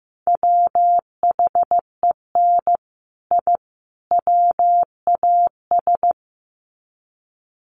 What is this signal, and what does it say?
Morse code, heard as a keyed tone, about 15 wpm, 700 Hz. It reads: WHEN I WAS
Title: Sets of 3 Words 15wpm